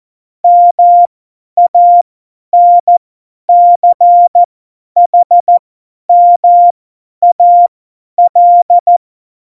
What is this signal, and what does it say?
Morse code, heard as a keyed tone, about 14 wpm, spaced as for 10 wpm, 700 Hz. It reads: MANCHMAL
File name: Morsekurs / Witze / Witz 07. Mai 2019, 50 F70